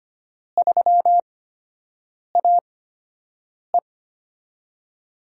Morse code, heard as a keyed tone, about 25 wpm, spaced as for 7 wpm, 700 Hz. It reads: 3AE
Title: USA Call Signs - 4 Characters - 8x Character Spacing - 25wpm